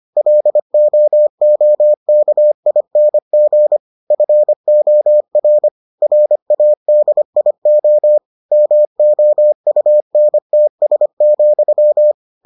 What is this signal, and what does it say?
Morse code, heard as a keyed tone, about 25 wpm, 600 Hz. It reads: LOOKING FOR RADIO MOUNTS,